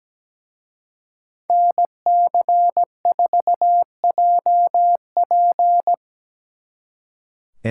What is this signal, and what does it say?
Morse code, heard as a keyed tone, about 20 wpm, 700 Hz. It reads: NC4JP
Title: Speed Racing - USA 2x2 Call Signs 17wpm